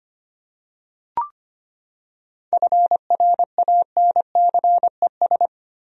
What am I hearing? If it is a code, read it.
Morse code, heard as a keyed tone, about 25 wpm, 700 Hz. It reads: FRANCES
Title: USA Common Names - 25wpm